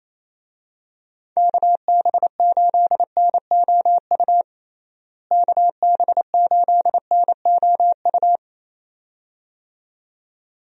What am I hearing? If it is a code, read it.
Morse code, heard as a keyed tone, about 28 wpm, 700 Hz. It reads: KB8NOU KB8NOU
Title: USA Call Signs - 6 Characters - Repeated 2x - 28wpm